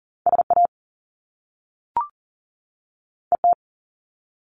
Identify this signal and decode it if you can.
Morse code, heard as a keyed tone, about 41 wpm, 700 Hz. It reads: SA ET
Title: Top Two Letter Combinations in Words 40wpm